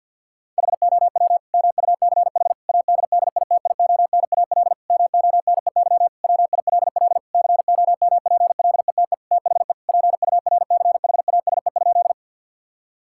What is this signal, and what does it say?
Morse code, heard as a keyed tone, about 52 wpm, 690 Hz. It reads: HOW MUCH ADDITIONAL MONEY WILL COMPLETE THE PURCHASE?
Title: Sentences from Top 1000 Words - Review Part 5 - 50wpm